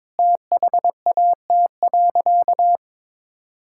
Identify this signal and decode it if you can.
Morse code, heard as a keyed tone, about 22 wpm, 700 Hz. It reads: THAT.
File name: practice / 22wpm / Sentences with 3-Word Phrases - 22wpm